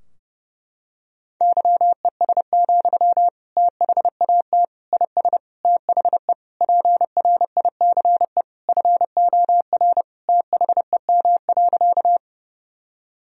Morse code, heard as a keyed tone, about 32 wpm, 700 Hz. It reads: YES, THAT IS THE PRICE FOR THEM.
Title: Sentences from Top 1000 Words - Review Part 4 - 30wpm